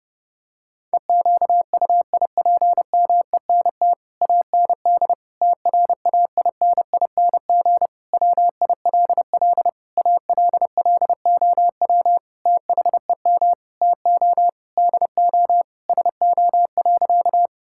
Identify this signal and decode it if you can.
Morse code, heard as a keyed tone, about 30 wpm, 700 Hz. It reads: EQUIPMENT AND TRAINING WILL ALLOW THEM TO DO SO.
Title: Sentences from Top 1000 Words - Review Part 5 - 30wpm